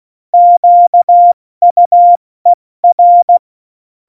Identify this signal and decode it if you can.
Morse code, heard as a keyed tone, about 16 wpm, spaced as for 14 wpm, 700 Hz. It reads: QUER
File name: Morsekurs / Witze / Witz 08. Dezember 2020, 70 F80